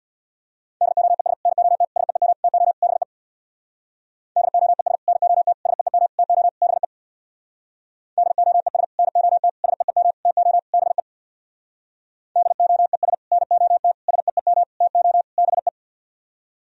Morse code, heard as a keyed tone, about 64 wpm, 700 Hz. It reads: DOES NOT SEEM TO BE DOES NOT SEEM TO BE DOES NOT SEEM TO BE DOES NOT SEEM TO BE